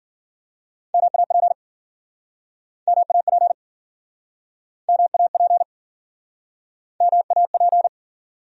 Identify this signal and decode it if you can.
Morse code, heard as a keyed tone, about 52 wpm, 700 Hz. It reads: MAP MAP MAP MAP